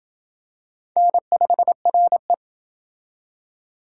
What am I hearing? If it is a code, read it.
Morse code, heard as a keyed tone, about 28 wpm, 700 Hz. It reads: N5RE